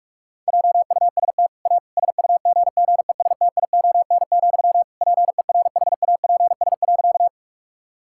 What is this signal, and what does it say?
Morse code, heard as a keyed tone, about 46 wpm, 700 Hz. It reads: JUST A SUGGESTION, PERHAPS.
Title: Sentences from Top 1600 Words 45wpm